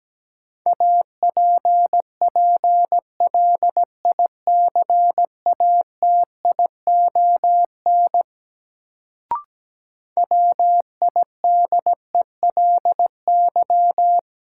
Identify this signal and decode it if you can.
Morse code, heard as a keyed tone, about 17 wpm, 700 Hz. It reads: APPLICATION WIDELY
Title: Sentences from Top 1200 Words - New Word Drill - 17wpm